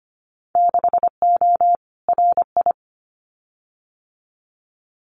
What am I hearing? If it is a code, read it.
Morse code, heard as a keyed tone, about 25 wpm, 700 Hz. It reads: 6O RI
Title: Winter Field Day Exchanges Encore 25wpm